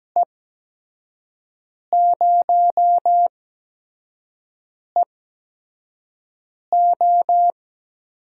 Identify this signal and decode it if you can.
Morse code, heard as a keyed tone, about 17 wpm, spaced as for 5 wpm, 700 Hz. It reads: E0EO
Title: USA Call Signs - 5 Characters - 8x Character Spacing - 17wpm